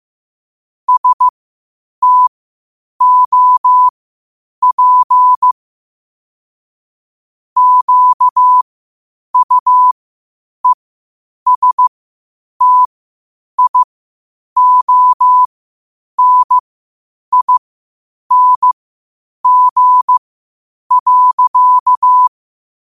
Morse code, heard as a keyed tone, about 15 wpm, spaced as for 8 wpm, 1000 Hz. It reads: STOP QUESTIONING.
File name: 8wpm